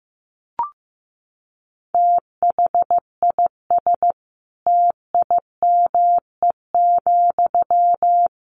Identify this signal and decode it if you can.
Morse code, heard as a keyed tone, about 15 wpm, 700 Hz. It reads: THIS TIME,
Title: A Sound Of Thunder 15wpm